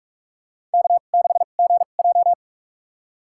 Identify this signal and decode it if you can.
Morse code, heard as a keyed tone, about 46 wpm, spaced as for 33 wpm, 700 Hz. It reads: K6GJ
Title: USA Call Signs - 2x Character Spacing - 45wpm